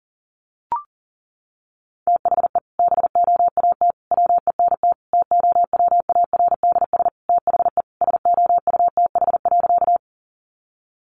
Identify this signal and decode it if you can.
Morse code, heard as a keyed tone, about 40 wpm, 700 Hz. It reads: THE BOAT WENT TOWARDS THE SOUTH.
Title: Sentences from Top 1100 Words 40wpm